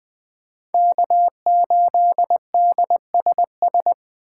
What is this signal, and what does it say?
Morse code, heard as a keyed tone, about 20 wpm, 700 Hz. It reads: K8DSS
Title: K1USN SST QSO Practice - 20wpm